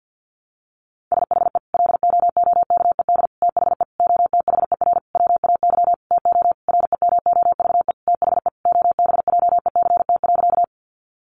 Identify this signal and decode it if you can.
Morse code, heard as a keyed tone, about 50 wpm, 700 Hz. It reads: SHE LOOKED THE OTHER WAY TO REMOVE THE OBJECT.
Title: Sentences from Top 800 Words 50wpm